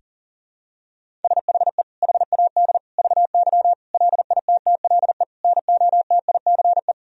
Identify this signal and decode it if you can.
Morse code, heard as a keyed tone, about 40 wpm, 700 Hz. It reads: SHE HAD VY LITTLE NOTICE
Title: Sentences from Top 400 Words and Abbreviations 40wpm